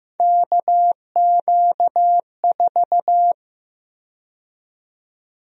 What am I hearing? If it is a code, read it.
Morse code, heard as a keyed tone, about 15 wpm, 700 Hz. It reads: KQ4
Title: USA Call Sign Prefixes - 15wpm